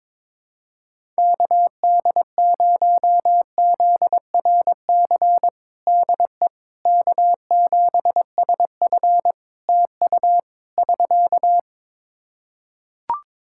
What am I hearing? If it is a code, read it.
Morse code, heard as a keyed tone, about 22 wpm, 700 Hz. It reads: KD0ZRC DE K7SF TU <SK>
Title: Practice QSO - Exchange 4 - Ending - 22wpm